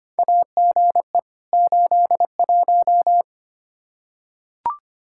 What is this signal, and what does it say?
Morse code, heard as a keyed tone, about 25 wpm, 700 Hz. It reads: AGE 81
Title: Practice QSO - Exchange 3 - More Info - 25wpm